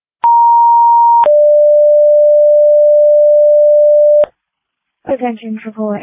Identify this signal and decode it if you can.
Two-tone page, heard as a one-second tone, then a three-second tone, about 940 then 590 Hz, with speech. It is heard from 0.2 to 4.3 s.